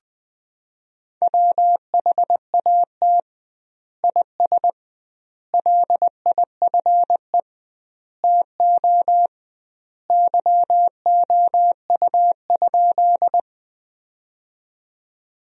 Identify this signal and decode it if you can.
Morse code, heard as a keyed tone, about 20 wpm, 700 Hz. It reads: WHAT IS LIFE TO YOU?
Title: 2x Word-Spacing - Sentences from Top 300 Words - 20wpm